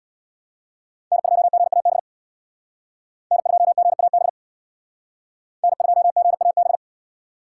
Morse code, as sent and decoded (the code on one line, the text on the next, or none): -. ..--- --. .- -... / -. ..--- --. .- -... / -. ..--- --. .- -...
N2GAB N2GAB N2GAB